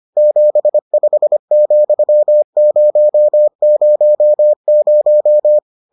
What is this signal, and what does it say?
Morse code, heard as a keyed tone, about 25 wpm, 600 Hz. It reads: 75,000